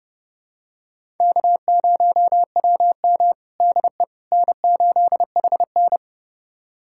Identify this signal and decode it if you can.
Morse code, heard as a keyed tone, about 29 wpm, 700 Hz. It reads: K0WM DE N8HN
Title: USA Callsign DE Callsign - 4 Characters - 30wpm